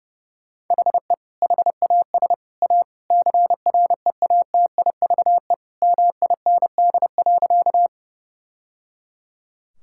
Morse code, heard as a keyed tone, about 28 wpm, 700 Hz. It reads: HE HAS A CREATIVE MIND.
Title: Sentences from Top 1000 Words - Review Part 5 - 30wpm